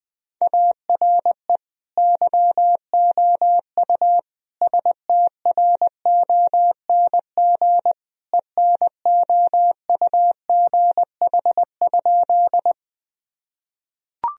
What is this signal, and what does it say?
Morse code, heard as a keyed tone, about 21 wpm, 700 Hz. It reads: ARE YOU STRONG ENOUGH?
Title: Sentences from Top 1000 Words - Review Part 3 - 20wpm